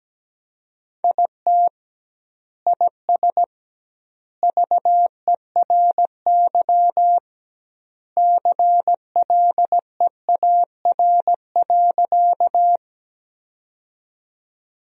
Morse code, heard as a keyed tone, about 17 wpm, 700 Hz. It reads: IT IS VERY CLEAR.
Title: 2x Word-Spacing - Sentences from Top 500 Words - 17wpm